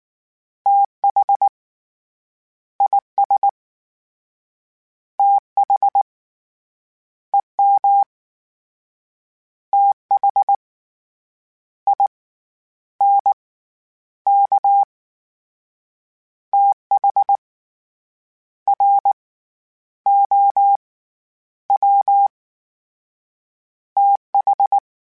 Morse code, heard as a keyed tone, about 19 wpm, 800 Hz. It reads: TH IS TH EM TH I N K TH R O W TH